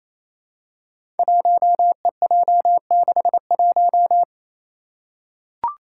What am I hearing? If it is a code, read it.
Morse code, heard as a keyed tone, about 28 wpm, 700 Hz. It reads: 1EJ61
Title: Five Challenging Letter-Number Combinations 28wpm